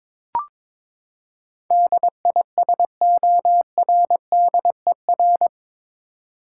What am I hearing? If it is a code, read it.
Morse code, heard as a keyed tone, about 22 wpm, 700 Hz. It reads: DISORDER
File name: practice / 22wpm / Eight Letter Words - 22wpm